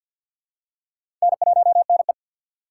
Morse code, heard as a keyed tone, about 49 wpm, 700 Hz. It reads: N1NE
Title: USA Call Signs - 4 Characters - 6x Character Spacing - 50wpm